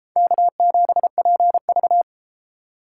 Morse code, heard as a keyed tone, about 33 wpm, 700 Hz. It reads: K7PV